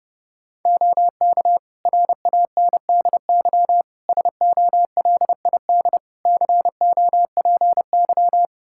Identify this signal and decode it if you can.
Morse code, heard as a keyed tone, about 30 wpm, 700 Hz. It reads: OK RANDY SOLID COPY
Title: Practice QSO - Exchange 3 - More Info - 30wpm